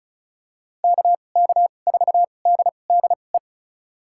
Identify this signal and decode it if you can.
Morse code, heard as a keyed tone, about 36 wpm, spaced as for 25 wpm, 700 Hz. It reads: KK4DDE
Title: USA Call Signs - 6 Characters - 2x Character Spacing - 35wpm